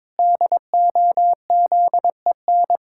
Morse code, heard as a keyed tone, about 21 wpm, 700 Hz. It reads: DOZEN
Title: Pangram Challenge v2 - 22wpm